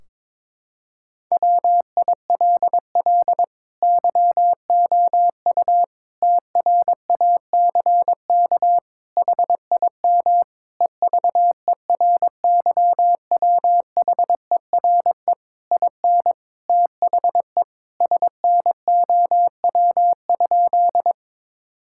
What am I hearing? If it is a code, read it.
Morse code, heard as a keyed tone, about 22 wpm, 700 Hz. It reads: WILL YOU TRACK HIM EVERYWHERE IN THE SNOW?
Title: Sentences from Top 1600 Words 22wpm